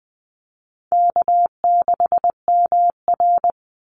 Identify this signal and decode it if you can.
Morse code, heard as a keyed tone, about 20 wpm, 700 Hz. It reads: K6MR
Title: Easy Call Signs Repeated 2x - 20wpm